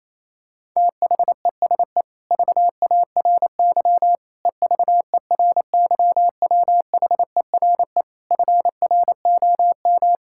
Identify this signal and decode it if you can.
Morse code, heard as a keyed tone, about 28 wpm, 700 Hz. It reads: THESE VARY EVERYWHERE FROM
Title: Sentences from Top 1600 Words 28wpm